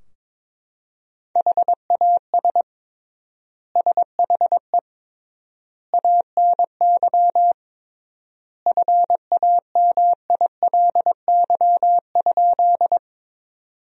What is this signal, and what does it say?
Morse code, heard as a keyed tone, about 22 wpm, 700 Hz. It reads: HAS SHE ANY FAMILY?
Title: 3x Word-Spacing - Sentences from Top 400 Words - 22wpm